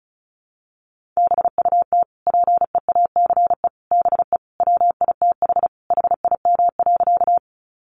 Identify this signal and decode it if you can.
Morse code, heard as a keyed tone, about 34 wpm, 700 Hz. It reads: BUT PEACE BE WITH HIM.
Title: Sentences from Top 1500 Words 35wpm